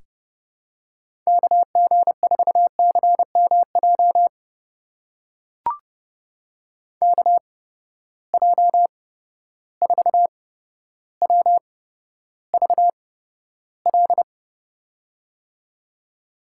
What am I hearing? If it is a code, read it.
Morse code, heard as a keyed tone, about 30 wpm, spaced as for 13 wpm, 700 Hz. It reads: KG4CMJ KJ4WVL
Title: USA Call Signs - 6 Characters - 8x Character Spacing - 30wpm